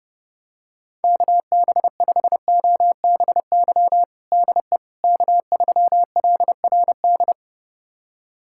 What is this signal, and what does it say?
Morse code, heard as a keyed tone, about 30 wpm, 700 Hz. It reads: KB5OBY DE K3LRD
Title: USA Callsign DE Callsign 30wpm